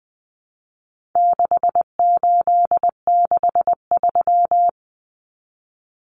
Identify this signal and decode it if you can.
Morse code, heard as a keyed tone, about 20 wpm, 700 Hz. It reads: 6863
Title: Common Cut Numbers - 4-Digit 20wpm